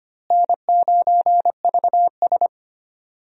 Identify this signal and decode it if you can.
Morse code, heard as a keyed tone, about 25 wpm, 700 Hz. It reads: N9VS
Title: USA Call Signs - 4 Characters - Repeated 2x - 25wpm